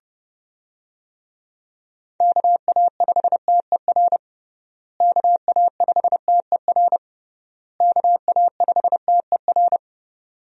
Morse code, heard as a keyed tone, about 30 wpm, 700 Hz. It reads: KA5TER KA5TER KA5TER
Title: USA Call Signs - 6 Characters - Repeated 3x - 30wpm